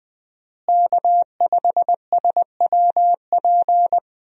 Morse code, heard as a keyed tone, about 20 wpm, 700 Hz. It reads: K5SWP
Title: USA Call Signs - 5 Characters - Repeated 2x - 20wpm